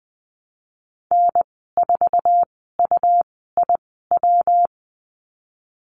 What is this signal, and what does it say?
Morse code, heard as a keyed tone, about 20 wpm, spaced as for 14 wpm, 700 Hz. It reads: N4UIW
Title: 154 Lesson 25g - Letter p - Call Signs 20wpm